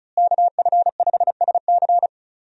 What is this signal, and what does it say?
Morse code, heard as a keyed tone, about 34 wpm, 700 Hz. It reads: KF5SC
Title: USA Call Signs - 5 Characters - Repeated 2x - 35wpm